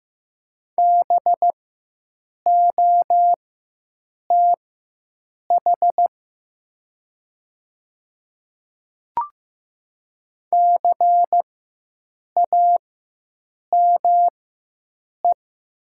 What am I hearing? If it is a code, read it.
Morse code, heard as a keyed tone, about 15 wpm, spaced as for 7 wpm, 700 Hz. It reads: BOTH CAME
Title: 4x Character Spacing - Top 300 Words - 15wpm